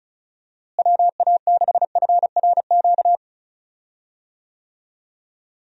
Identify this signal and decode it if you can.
Morse code, heard as a keyed tone, about 36 wpm, 700 Hz. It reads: WA6FRQ
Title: USA Call Signs - 6 Characters - Repeated 2x - 35wpm